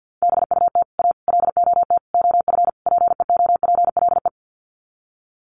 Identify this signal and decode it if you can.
Morse code, heard as a keyed tone, about 48 wpm, 700 Hz. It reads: BUT A LOT OF PEOPLE
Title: Sets of 5 Words Encore2 50wpm